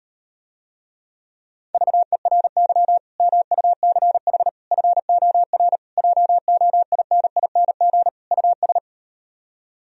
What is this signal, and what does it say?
Morse code, heard as a keyed tone, about 38 wpm, 700 Hz. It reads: VERY MUCH FOR JOINING US